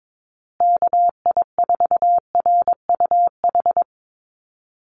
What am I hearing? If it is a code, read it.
Morse code, heard as a keyed tone, about 22 wpm, 700 Hz. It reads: KI4RUH
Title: USA Call Signs Repeated 2x 22wpm